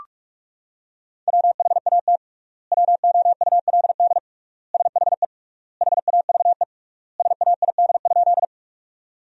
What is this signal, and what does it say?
Morse code, heard as a keyed tone, about 46 wpm, 690 Hz. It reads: WHAT WOULD SHE HAVE SAID?
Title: 3x Word-Spacing - Sentences from Top 100 Words - 45wpm